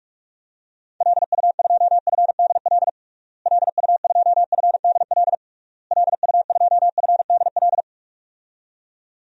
Morse code, heard as a keyed tone, about 46 wpm, 700 Hz. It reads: LU2FDL LU2FDL LU2FDL
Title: DXCC-100 Call Signs - 6 Characters - Repeated 3x - 45wpm